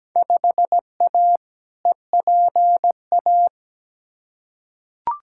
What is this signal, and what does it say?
Morse code, heard as a keyed tone, about 17 wpm, 700 Hz. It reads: 5A EPA
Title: Field Day Exchanges - 17wpm